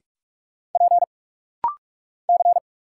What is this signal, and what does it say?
Morse code, heard as a keyed tone, about 44 wpm, 700 Hz. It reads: P C
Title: Single Letter-Number Alt - Rapid-Fire 45wpm